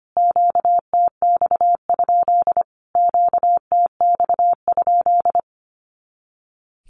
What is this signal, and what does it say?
Morse code, heard as a keyed tone, about 25 wpm, 700 Hz. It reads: QTX? QTX?
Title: Q-Codes Repeated 3x 25wpm